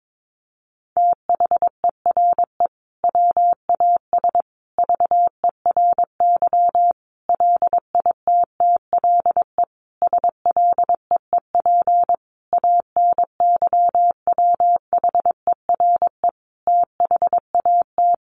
Morse code, heard as a keyed tone, about 22 wpm, 700 Hz. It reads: THERE WAS VERY LITTLE SLEEP ANYWHERE THAT
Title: Sentences from Top 1400 Words 22wpm